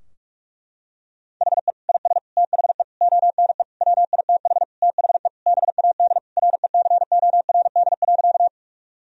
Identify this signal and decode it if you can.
Morse code, heard as a keyed tone, about 43 wpm, 700 Hz. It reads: HE IS THE ONE WITH THE BAD RECORD.
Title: Sentences from Top 1000 Words - Review Part 3 - 45wpm